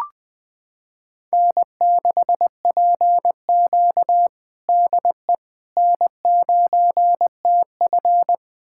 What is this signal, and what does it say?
Morse code, heard as a keyed tone, about 20 wpm, 700 Hz. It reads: N6PQ DE N9TF
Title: USA Callsign DE Callsign - 4 Characters - 20wpm